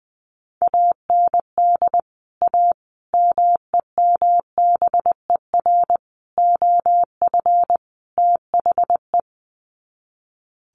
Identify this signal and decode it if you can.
Morse code, heard as a keyed tone, about 20 wpm, 700 Hz. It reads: AND A MEMBER OF THE
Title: Sets of 5 Words 20wpm